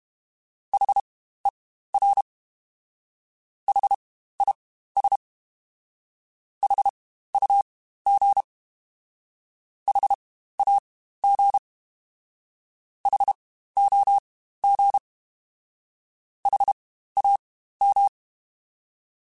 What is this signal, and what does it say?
Morse code, heard as a keyed tone, about 32 wpm, spaced as for 15 wpm, 800 Hz. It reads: HER HIS HUG HAG HOG HAM
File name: Spaced Alphas / Spaced Alphas mp3s / Hs sp @30